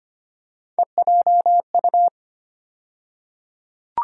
Morse code, heard as a keyed tone, about 25 wpm, 700 Hz. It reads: EJU
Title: USA Call Sign Postfixes - 3 Letters 25wpm